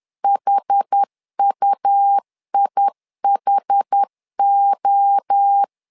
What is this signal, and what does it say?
Morse code, heard as a keyed tone, about 10 wpm, 800 Hz. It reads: HUIHO
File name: LEZIONI / 4 Q 10wpm 2210